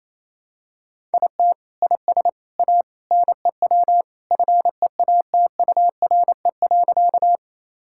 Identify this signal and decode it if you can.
Morse code, heard as a keyed tone, about 28 wpm, 700 Hz. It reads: IT IS A NEW FEATURE.
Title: Sentences from Top 1000 Words - Review Part 5 - 28wpm